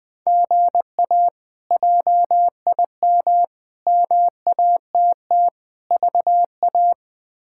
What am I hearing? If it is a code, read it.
Morse code, heard as a keyed tone, about 20 wpm, 700 Hz. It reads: GA JIM MATT VA
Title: K1USN SST QSO Practice - 20wpm